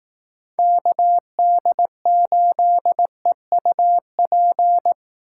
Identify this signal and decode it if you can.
Morse code, heard as a keyed tone, about 18 wpm, 700 Hz. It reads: KD8EUP
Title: Speed Racing - USA 2x3 Call Signs 15wpm